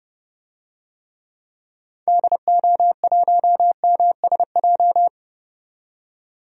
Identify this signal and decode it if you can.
Morse code, heard as a keyed tone, about 30 wpm, 700 Hz. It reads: DO1MSJ
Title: DXCC-100 Call Signs - 6 Characters - 30wpm